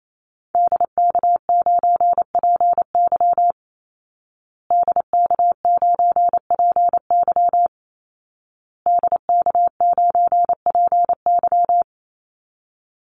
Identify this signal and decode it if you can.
Morse code, heard as a keyed tone, about 28 wpm, 700 Hz. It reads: DK9PY DK9PY DK9PY
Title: Call Signs Repeated 2x - 28wpm